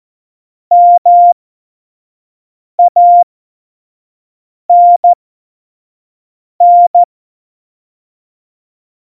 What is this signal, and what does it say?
Morse code, heard as a keyed tone, about 14 wpm, spaced as for 5 wpm, 700 Hz. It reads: MANN
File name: Morsekurs / Witze / Witz 06. November 2018, 25 F70